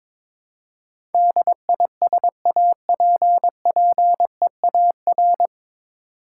Morse code, heard as a keyed tone, about 22 wpm, 690 Hz. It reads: DISAPPEAR
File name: 9 Letter Words - First 3 Letters Given - 22wpm